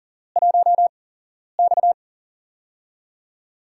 Morse code, heard as a keyed tone, about 40 wpm, spaced as for 11 wpm, 700 Hz. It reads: 1X
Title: USA Call Signs - 4 Characters - 8x Character Spacing - 40wpm